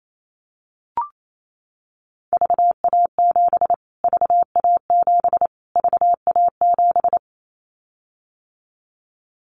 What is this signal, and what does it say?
Morse code, heard as a keyed tone, about 28 wpm, 700 Hz. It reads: VA7 VA7 VA7
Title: Callsign Prefixes for Top 25 DXCC - Repeated 3x - 28wpm